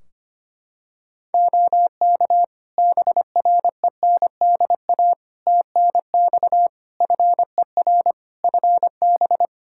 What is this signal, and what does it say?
Morse code, heard as a keyed tone, about 25 wpm, 700 Hz. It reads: OK BRENDA TNX FER FB
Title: Practice QSO - Exchange 4 - Ending - 25wpm